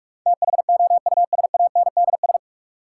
Morse code, heard as a keyed tone, about 42 wpm, 700 Hz. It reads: THOUSANDS
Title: Sets of 4 Words - New Word Drill - 45wpm